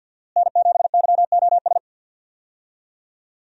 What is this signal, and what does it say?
Morse code, heard as a keyed tone, about 52 wpm, 700 Hz. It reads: N7YOS
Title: USA Call Signs - 5 Characters - Repeated 2x - 50wpm